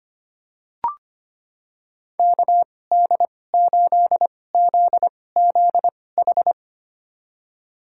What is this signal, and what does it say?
Morse code, heard as a keyed tone, about 25 wpm, spaced as for 18 wpm, 700 Hz. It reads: KD8ZZH
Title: USA Call Signs - 6 Characters - 2x Character Spacing - 25wpm